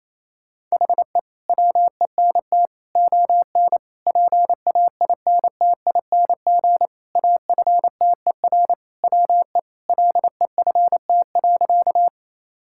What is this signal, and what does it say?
Morse code, heard as a keyed tone, about 27 wpm, 700 Hz. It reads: HE WENT ON PAINTING AFTER WE LEFT.
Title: Sentences from Top 1000 Words - Review Part 6 - 28wpm